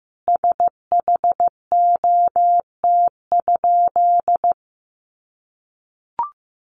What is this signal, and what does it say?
Morse code, heard as a keyed tone, about 14 wpm, 700 Hz. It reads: SHOT?
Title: Sentences from Top 1100 Words 15wpm